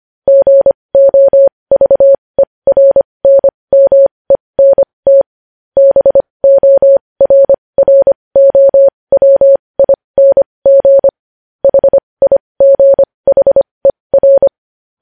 Morse code, heard as a keyed tone, about 25 wpm, spaced as for 20 wpm, 550 Hz. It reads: GOVERNMENT BORROWING HIGHER
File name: podcast / News headlines-1x-25x20